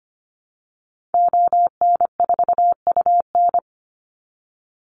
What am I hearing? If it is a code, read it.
Morse code, heard as a keyed tone, about 25 wpm, 700 Hz. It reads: ON4UN
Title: Call Signs Repeated 3x - 25wpm